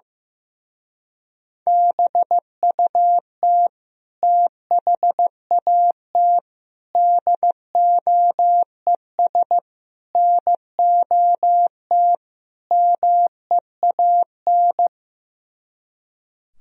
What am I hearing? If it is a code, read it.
Morse code, heard as a keyed tone, about 15 wpm, 700 Hz. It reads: BUT THAT DOES NOT MEAN
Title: Speed-Racing - Sets of 5 Words 15wpm